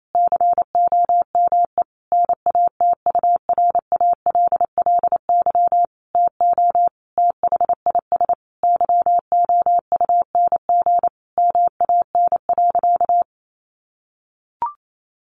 Morse code, heard as a keyed tone, about 28 wpm, 700 Hz. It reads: COME NATURALLY TO THIS YOUNG MAN.
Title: Sentences from Top 1300 Words 28wpm